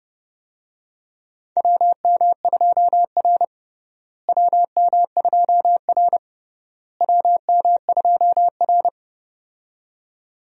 Morse code, heard as a keyed tone, about 32 wpm, 700 Hz. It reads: WM2R WM2R WM2R
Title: USA Call Signs - 4 Characters - Repeated 3x - 30wpm